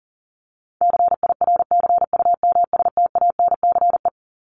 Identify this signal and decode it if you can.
Morse code, heard as a keyed tone, about 40 wpm, 700 Hz. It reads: CIRCUMSTANCE
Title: ES:S Suffix - Top 100 Words with Base Words 40wpm